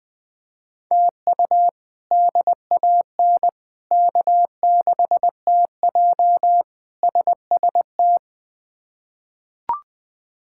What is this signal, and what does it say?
Morse code, heard as a keyed tone, about 21 wpm, 700 Hz. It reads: TU DAN K6TJ SST
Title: K1USN SST QSO Practice - 20wpm